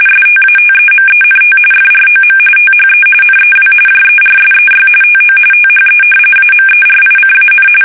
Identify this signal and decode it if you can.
RTTY, about 45.45 baud. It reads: STATEMENT FROM THE NEW JERSEY TO THE AMERICAN
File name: fsk850 / battleshipnj